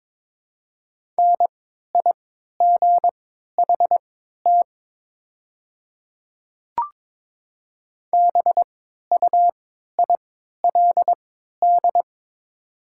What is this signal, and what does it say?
Morse code, heard as a keyed tone, about 22 wpm, spaced as for 12 wpm, 700 Hz. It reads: NIGHT BUILD
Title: 3x Character Spacing - Top 500 Words - 22wpm